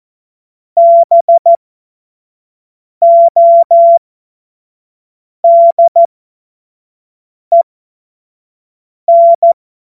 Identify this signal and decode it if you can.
Morse code, heard as a keyed tone, about 14 wpm, spaced as for 5 wpm, 680 Hz. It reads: BODEN
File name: Morsekurs / Witze / Witz 19 Dezember 2023 25 F70